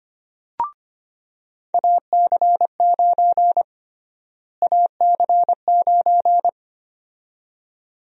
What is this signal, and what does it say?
Morse code, heard as a keyed tone, about 25 wpm, 700 Hz. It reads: AC9 AC9
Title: USA Call Sign Prefixes - Repeated 2x - 25wpm